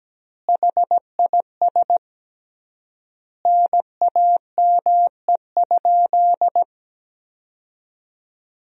Morse code, heard as a keyed tone, about 17 wpm, 700 Hz. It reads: HIS NAME?
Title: 3x Word-Spacing - Sentences from Top 400 Words - 17wpm